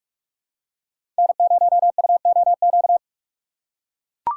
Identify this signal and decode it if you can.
Morse code, heard as a keyed tone, about 44 wpm, 700 Hz. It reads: N0UOQ
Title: USA Call Signs - 5 Characters - 6x Character Spacing - 45wpm